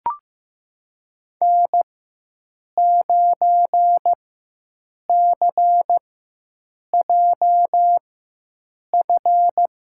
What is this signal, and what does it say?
Morse code, heard as a keyed tone, about 15 wpm, spaced as for 7 wpm, 700 Hz. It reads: N9CJF
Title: USA Call Signs - 5 Characters - 4x Character Spacing - 15wpm